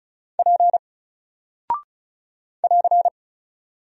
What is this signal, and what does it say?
Morse code, heard as a keyed tone, about 36 wpm, 700 Hz. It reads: P <AR>
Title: Single Character (Letter:Number:Punctuation:Prosign) Alt 35wpm